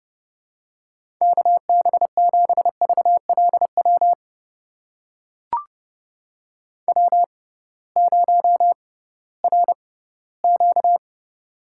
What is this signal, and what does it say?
Morse code, heard as a keyed tone, about 30 wpm, spaced as for 17 wpm, 700 Hz. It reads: KB7VLW W0RQ
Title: USA Call Signs - 6x Character Spacing - 30wpm